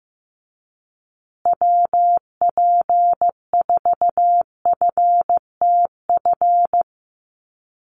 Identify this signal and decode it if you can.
Morse code, heard as a keyed tone, about 15 wpm, 700 Hz. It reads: WP4FTF
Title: USA Call Signs Repeated 3x 15wpm